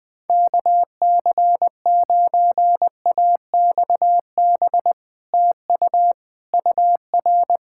Morse code, heard as a keyed tone, about 20 wpm, 700 Hz. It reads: KC9AXB TU UR
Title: Speed-Racing Common QSO Phrases - 20wpm